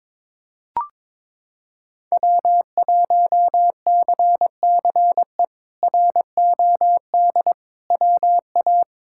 Morse code, heard as a keyed tone, about 22 wpm, 700 Hz. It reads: W1CCE ROD WA
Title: K1USN SST QSO Practice - 22wpm